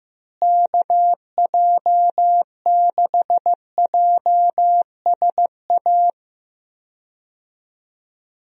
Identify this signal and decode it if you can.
Morse code, heard as a keyed tone, about 15 wpm, 700 Hz. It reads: KJ6JSA
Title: USA Call Signs - 6 Characters - Repeated 3x - 15wpm